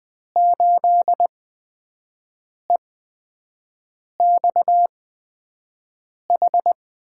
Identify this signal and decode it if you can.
Morse code, heard as a keyed tone, about 20 wpm, spaced as for 5 wpm, 700 Hz. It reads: 8EXH
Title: USA Call Signs - 5 Characters - 8x Character Spacing - 20wpm